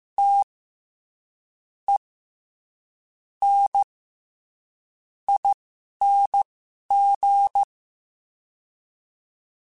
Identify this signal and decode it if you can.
Morse code, heard as a keyed tone, about 15 wpm, spaced as for 6 wpm, 800 Hz. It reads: TENING
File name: Special INGs-mp3s / Special INGs-mp3s / SlowToFaster ing @10